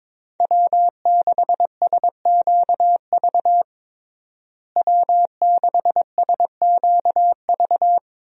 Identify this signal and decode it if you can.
Morse code, heard as a keyed tone, about 21 wpm, 700 Hz. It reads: W6SQV W6SQV
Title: USA Call Signs - 5 Characters - Repeated 3x - 22wpm